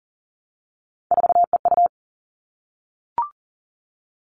Morse code, heard as a keyed tone, about 40 wpm, 700 Hz. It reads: 4EU
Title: All Cut Numbers - 3-Digit 40wpm